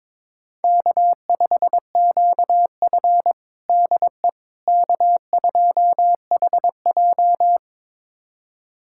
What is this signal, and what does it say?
Morse code, heard as a keyed tone, about 21 wpm, 700 Hz. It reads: K5QF DE K2HJ
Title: USA Callsign DE Callsign - 4 Characters - 22wpm